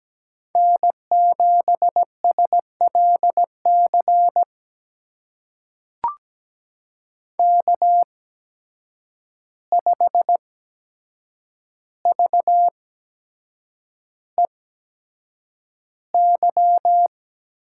Morse code, heard as a keyed tone, about 17 wpm, spaced as for 7 wpm, 700 Hz. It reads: N7SLC K5VEY
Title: USA Call Signs - 5 Characters - 8x Character Spacing - 17wpm